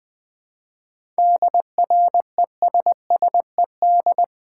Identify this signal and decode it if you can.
Morse code, heard as a keyed tone, about 20 wpm, 700 Hz. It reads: DRESSED
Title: 7 Letter Words - First 3 Letters Given - 20wpm